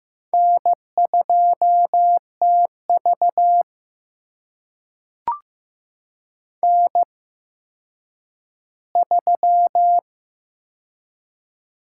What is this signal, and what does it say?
Morse code, heard as a keyed tone, about 16 wpm, spaced as for 9 wpm, 700 Hz. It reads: N2TV N3
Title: USA Call Signs - 4 Characters - 8x Character Spacing - 15wpm